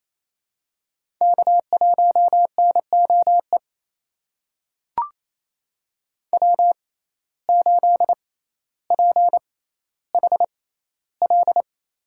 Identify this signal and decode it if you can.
Morse code, heard as a keyed tone, about 28 wpm, spaced as for 14 wpm, 700 Hz. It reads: K1NOE W8PHL
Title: USA Call Signs - 5 Characters - 6x Character Spacing - 28wpm